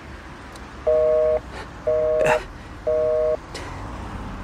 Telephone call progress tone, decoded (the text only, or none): busy tone